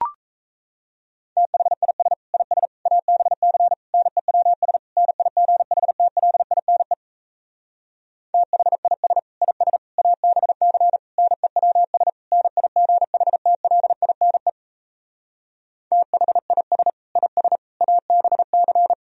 Morse code, heard as a keyed tone, about 38 wpm, 700 Hz. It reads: THIS IS ABC NEWS NIGHTLINE THIS IS ABC NEWS NIGHTLINE THIS IS ABC